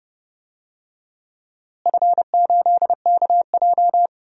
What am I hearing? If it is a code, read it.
Morse code, heard as a keyed tone, about 30 wpm, 700 Hz. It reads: F8KJ
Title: DXCC-100 Call Signs - 4 Characters - Repeated 3x - 30wpm